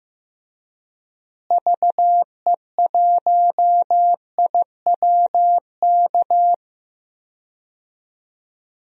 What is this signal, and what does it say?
Morse code, heard as a keyed tone, about 16 wpm, 700 Hz. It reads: VE1IWK
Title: DXCC-100 Call Signs - 6 Characters - Repeated 2x - 15wpm